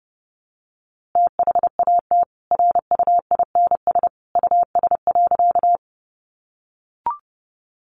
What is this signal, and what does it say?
Morse code, heard as a keyed tone, about 30 wpm, 700 Hz. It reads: THAT RUINS US.
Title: A Sound Of Thunder 30wpm